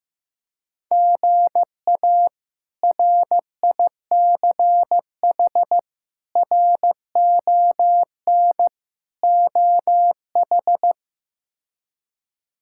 Morse code, heard as a keyed tone, about 15 wpm, 700 Hz. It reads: GA RICH RON OH
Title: K1USN SST QSO Practice - 15wpm